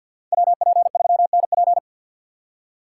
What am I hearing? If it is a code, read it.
Morse code, heard as a keyed tone, about 58 wpm, 700 Hz. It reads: WP3NP
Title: Speed Racing - USA 2x2 Call Signs 50wpm